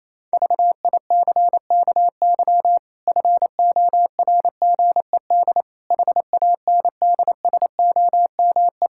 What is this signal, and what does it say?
Morse code, heard as a keyed tone, about 27 wpm, 700 Hz. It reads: VICKY FORGED HANDSOME
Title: Pangram Challenge v2 - 28wpm